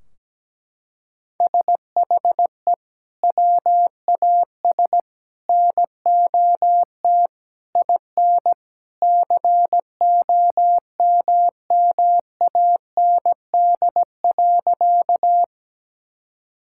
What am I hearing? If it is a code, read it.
Morse code, heard as a keyed tone, about 17 wpm, 700 Hz. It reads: SHE WAS NOT IN COMMAND.
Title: Sentences from Top 1600 Words 17wpm